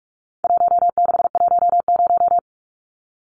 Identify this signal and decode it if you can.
Morse code, heard as a keyed tone, about 44 wpm, 700 Hz. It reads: 1610
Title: Numbers - 4-Digit 45wpm